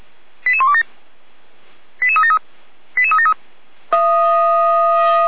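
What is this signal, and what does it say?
Five-tone selective call (ZVEI), heard as quick runs of five tones, tones of about 70 ms. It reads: 88127 88373 88373
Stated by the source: ZVEI five-tone selective call received by SDR.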